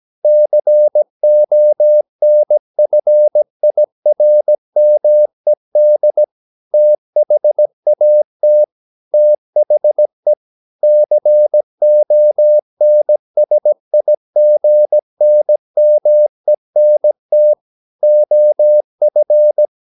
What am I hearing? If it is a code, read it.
Morse code, heard as a keyed tone, about 17 wpm, 600 Hz. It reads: CONFIRMED THAT THE CONSIGNMENT OF
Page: Romanen Never af Ken Follet på engelsk.